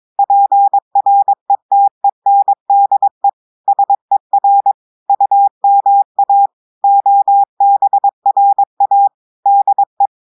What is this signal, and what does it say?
Morse code, heard as a keyed tone, about 22 wpm, 800 Hz. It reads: PRETENDE SER UMA OBRA DE
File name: audios / portugais / portugais texte en clair 1 vitesse 22 mots minutes